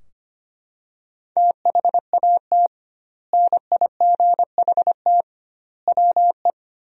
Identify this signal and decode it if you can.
Morse code, heard as a keyed tone, about 25 wpm, 700 Hz. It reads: THAT NIGHT WE
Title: 2x Word-Spacing - Sentences from Top 300 Words - 25wpm